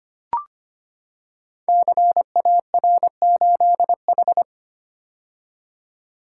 Morse code, heard as a keyed tone, about 25 wpm, 700 Hz. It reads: CAR8H
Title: Code Groups - 5 Letter:Numbers - 25wpm